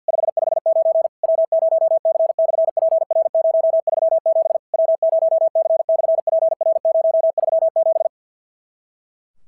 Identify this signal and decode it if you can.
Morse code, heard as a keyed tone, about 50 wpm, 650 Hz. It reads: 559 W0C/PR037 W0C/PR037
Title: SOTA QSO Practice 50wpm